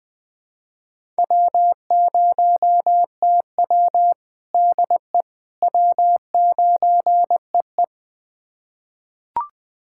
Morse code, heard as a keyed tone, about 20 wpm, 700 Hz. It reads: W0TW DE W9EE
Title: USA Callsign DE Callsign - 4 Characters - 20wpm